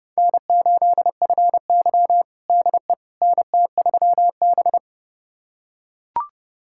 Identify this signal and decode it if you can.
Morse code, heard as a keyed tone, about 30 wpm, 700 Hz. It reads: N8FY DE NT3B
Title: USA Callsign DE Callsign - 4 Characters - 30wpm